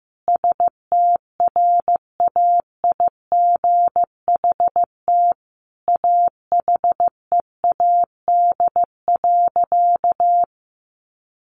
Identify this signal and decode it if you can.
Morse code, heard as a keyed tone, about 15 wpm, 700 Hz. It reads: STRAIGHT AHEAD.
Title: Sentences from Top 1100 Words 15wpm